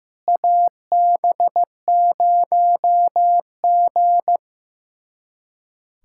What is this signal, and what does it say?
Morse code, heard as a keyed tone, about 15 wpm, 700 Hz. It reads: AB0G
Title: USA Callsign DE Callsign - 4 Characters - 15wpm